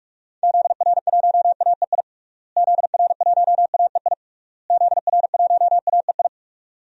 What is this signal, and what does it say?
Morse code, heard as a keyed tone, about 47 wpm, 700 Hz. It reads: ZR1AEI ZR1AEI ZR1AEI
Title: DXCC-100 Call Signs - 6 Characters - Repeated 3x - 45wpm